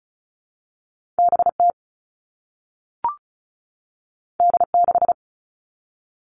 Morse code, heard as a keyed tone, about 36 wpm, 700 Hz. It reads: BT D6
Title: All Cut Numbers - 2-Digit 35wpm